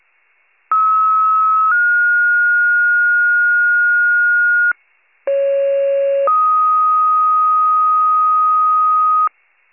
Two-tone page, heard as a one-second tone, then a three-second tone, about 1290 then 1430 Hz, about 550 then 1250 Hz. It is heard from 0.7 to 9.3 s.
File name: pager / 84 FIRE / wcalert